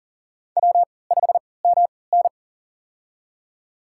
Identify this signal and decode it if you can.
Morse code, heard as a keyed tone, about 37 wpm, spaced as for 22 wpm, 700 Hz. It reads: W5MN
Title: USA Call Signs - 3x Character Spacing - 40wpm